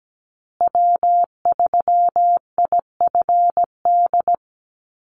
Morse code, heard as a keyed tone, about 17 wpm, 700 Hz. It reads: W3IFD
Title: USA Call Signs Repeated 2x 17wpm